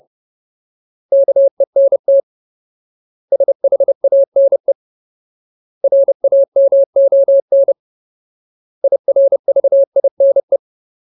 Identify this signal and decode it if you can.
Morse code, heard as a keyed tone, about 30 wpm, 550 Hz. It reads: KENT SHANE RAMON IRVINE
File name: Ultimate Word List / 30WPM Ultimate